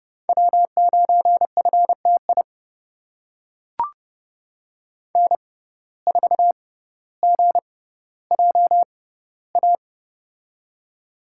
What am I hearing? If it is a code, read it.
Morse code, heard as a keyed tone, about 31 wpm, spaced as for 15 wpm, 700 Hz. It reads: W9FTI N4GJA
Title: USA Call Signs - 5 Characters - 6x Character Spacing - 30wpm